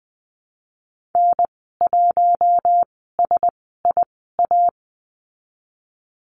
Morse code, heard as a keyed tone, about 20 wpm, spaced as for 15 wpm, 700 Hz. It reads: N1SIA
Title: 082 Lesson 14g - Number 2 - Call Signs 20wpm